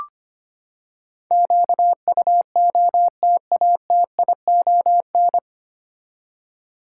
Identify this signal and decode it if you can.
Morse code, heard as a keyed tone, about 25 wpm, 700 Hz. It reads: QUOTATION
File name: Nine Letter Words - 25wpm